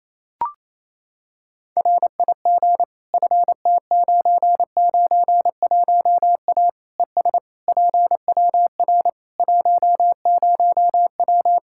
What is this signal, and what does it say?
Morse code, heard as a keyed tone, about 27 wpm, 700 Hz. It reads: RIG FT991A ES PWR 10W
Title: Practice QSO - Exchange 2 - Info - 28wpm